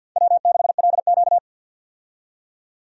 Morse code, heard as a keyed tone, about 53 wpm, 700 Hz. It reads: W6LQ
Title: USA Call Signs - 4 Characters - 50wpm